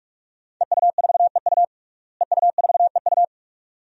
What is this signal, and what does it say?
Morse code, heard as a keyed tone, about 48 wpm, 700 Hz. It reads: EU4EU EU4EU
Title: DXCC-100 Call Signs - 5 Characters - Repeated 2x - 45wpm